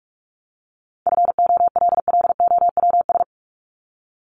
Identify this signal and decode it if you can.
Morse code, heard as a keyed tone, about 44 wpm, 700 Hz. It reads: FOLLOWS
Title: ES:S Suffix - Top 100 Words with Base Words 45wpm